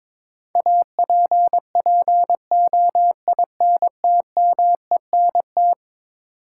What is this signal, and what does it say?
Morse code, heard as a keyed tone, about 22 wpm, 700 Hz. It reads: APPOINTMENT
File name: practice / 22wpm / Eleven Letter Words - 22wpm